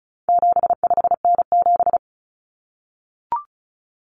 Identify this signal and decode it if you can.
Morse code, heard as a keyed tone, about 34 wpm, 700 Hz. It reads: 75N7
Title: Common Cut Numbers - 4-Digit 35wpm